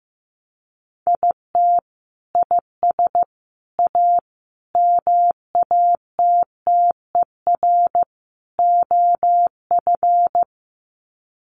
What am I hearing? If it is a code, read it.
Morse code, heard as a keyed tone, about 15 wpm, 700 Hz. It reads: IT IS A MATTER OF
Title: Sets of 5 Words Encore2 15wpm